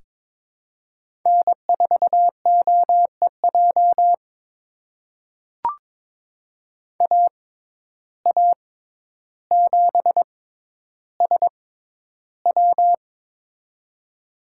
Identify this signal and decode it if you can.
Morse code, heard as a keyed tone, about 22 wpm, spaced as for 11 wpm, 700 Hz. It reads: N4OEJ AA7SW
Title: USA Call Signs - 5 Characters - 6x Character Spacing - 22wpm